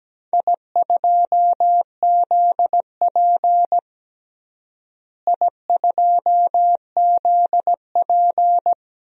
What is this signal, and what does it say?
Morse code, heard as a keyed tone, about 17 wpm, 700 Hz. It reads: I2ZP I2ZP
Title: DXCC-100 Call Signs - 4 Characters - Repeated 3x - 17wpm